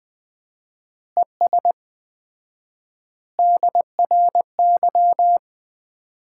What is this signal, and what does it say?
Morse code, heard as a keyed tone, about 20 wpm, 700 Hz. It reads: ES DRY
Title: Common QSO Phrases - 4x Word-Spacing - 20wpm